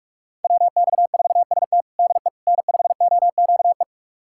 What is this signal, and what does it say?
Morse code, heard as a keyed tone, about 42 wpm, 700 Hz. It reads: WX4ST DE N5OQE
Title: USA Callsign DE Callsign 45wpm